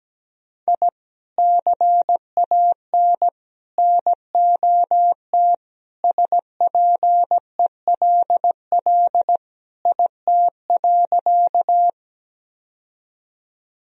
Morse code, heard as a keyed tone, about 17 wpm, 700 Hz. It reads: I CAN NOT SPELL IT.
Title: Sentences from Top 1000 Words - Review Part 1 - 17wpm